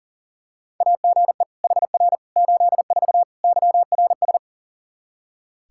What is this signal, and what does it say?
Morse code, heard as a keyed tone, about 40 wpm, 700 Hz. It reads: AGE HR 84 YRS
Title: Practice QSO - Exchange 3 - More Info - 40wpm